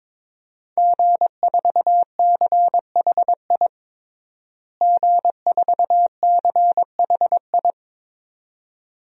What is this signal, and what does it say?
Morse code, heard as a keyed tone, about 22 wpm, 700 Hz. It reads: G4CHI G4CHI
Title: DXCC-100 Call Signs - 5 Characters - Repeated 2x - 22wpm